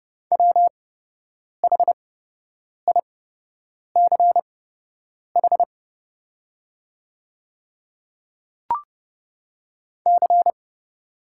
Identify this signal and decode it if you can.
Morse code, heard as a keyed tone, about 30 wpm, spaced as for 8 wpm, 700 Hz. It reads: WHICH C
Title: 8x Character Spacing - Top 200 Words - 30wpm